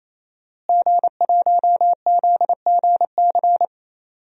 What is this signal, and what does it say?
Morse code, heard as a keyed tone, about 27 wpm, 700 Hz. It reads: G1ZGC
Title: DXCC-100 Call Signs - 5 Characters - Repeated 3x - 28wpm